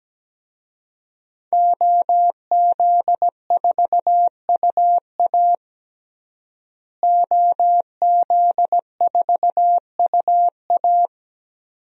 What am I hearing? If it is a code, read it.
Morse code, heard as a keyed tone, about 17 wpm, 700 Hz. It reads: OZ4UA OZ4UA
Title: DXCC-100 Call Signs - 5 Characters - Repeated 2x - 17wpm